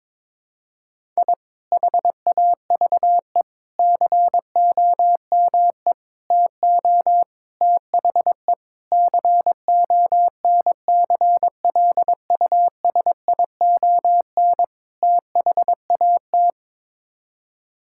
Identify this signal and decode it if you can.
Morse code, heard as a keyed tone, about 21 wpm, 700 Hz. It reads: I HAVE COME TO THE CONCLUSION THAT
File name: practice / 22wpm / Sets of 7 Words - 22wpm